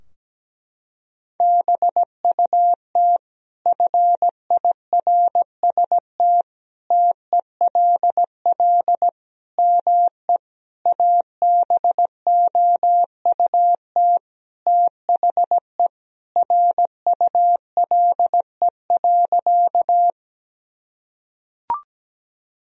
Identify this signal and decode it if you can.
Morse code, heard as a keyed tone, about 17 wpm, 700 Hz. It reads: BUT FIRST TELL ME ABOUT THE RULE.
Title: Sentences from Top 1000 Words - Review Part 2 - 17wpm